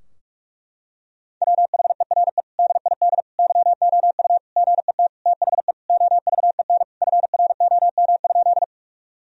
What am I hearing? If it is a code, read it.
Morse code, heard as a keyed tone, about 41 wpm, 700 Hz. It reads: WHERE DID YOU GET THE OVEN FROM?
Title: Sentences from Top 1000 Words - Review Part 4 - 45wpm